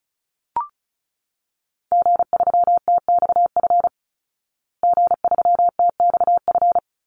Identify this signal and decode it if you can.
Morse code, heard as a keyed tone, about 34 wpm, 700 Hz. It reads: G3TXF G3TXF
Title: Call Signs Repeated 3x - 35wpm